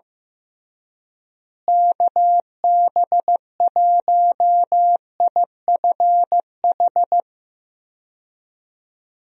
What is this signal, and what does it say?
Morse code, heard as a keyed tone, about 15 wpm, 700 Hz. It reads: KB1IFH
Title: USA Call Signs - 6 Characters - Repeated 3x - 15wpm